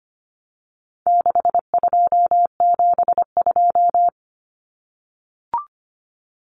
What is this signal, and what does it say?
Morse code, heard as a keyed tone, about 25 wpm, 700 Hz. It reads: 6272
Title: Numbers - 4-Digit 25wpm